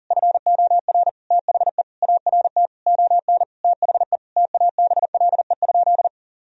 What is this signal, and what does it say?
Morse code, heard as a keyed tone, about 37 wpm, 700 Hz. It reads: FOR THE ART ON THE TABLE?
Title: Sentences from Top 1000 Words - Review Part 3 - 40wpm